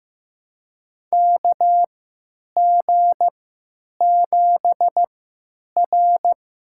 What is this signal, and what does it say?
Morse code, heard as a keyed tone, about 15 wpm, spaced as for 9 wpm, 700 Hz. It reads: KG7R
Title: USA Call Signs - 6 Characters - 3x Character Spacing - 15wpm